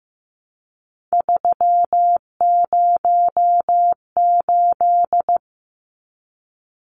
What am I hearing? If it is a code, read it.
Morse code, heard as a keyed tone, about 15 wpm, 700 Hz. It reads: 308
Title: Numbers - 3-Digit 15wpm